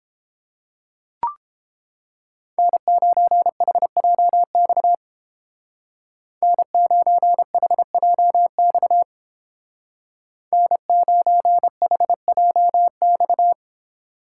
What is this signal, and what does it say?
Morse code, heard as a keyed tone, about 30 wpm, 700 Hz. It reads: N9HJX N9HJX N9HJX